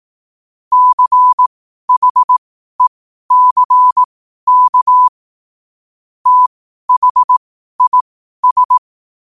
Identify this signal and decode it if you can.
Morse code, heard as a keyed tone, about 18 wpm, spaced as for 12 wpm, 1000 Hz. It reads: CHECK THIS